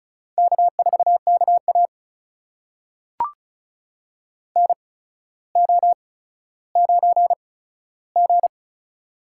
Morse code, heard as a keyed tone, about 36 wpm, spaced as for 15 wpm, 700 Hz. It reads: K4KA NO9G
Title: USA Call Signs - 4 Characters - 8x Character Spacing - 35wpm